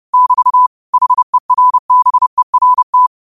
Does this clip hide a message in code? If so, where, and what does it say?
Yes: Morse code from 0.1 to 3.1 s, heard as a keyed tone, about 30 wpm, 1000 Hz. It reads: <BT> HERBERT